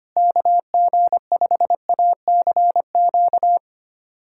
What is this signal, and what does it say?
Morse code, heard as a keyed tone, about 25 wpm, 700 Hz. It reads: KG5ACQ
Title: USA Callsign DE Callsign - 6 Characters - 25wpm